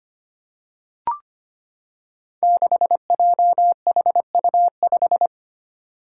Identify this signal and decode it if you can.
Morse code, heard as a keyed tone, about 25 wpm, 700 Hz. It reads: 6JHU5
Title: Five Challenging Letter-Number Combinations 25wpm